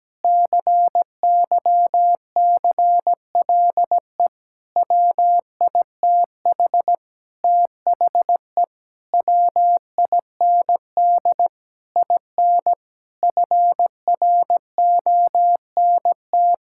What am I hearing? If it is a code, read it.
Morse code, heard as a keyed tone, about 17 wpm, 700 Hz. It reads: CYCLE WITH THE WIND IN FRONT
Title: Sentences from Top 1000 Words - Review Part 6 - 17wpm